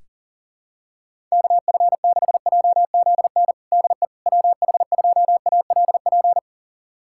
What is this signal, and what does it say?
Morse code, heard as a keyed tone, about 37 wpm, 700 Hz. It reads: KF6JZN DE WH2ALP
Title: USA Callsign DE Callsign - 6 Characters - 40wpm